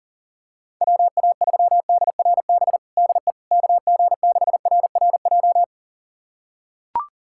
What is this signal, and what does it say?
Morse code, heard as a keyed tone, about 37 wpm, 700 Hz. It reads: WA3DRB DE KG6RRJ
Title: USA Callsign DE Callsign - 6 Characters - 40wpm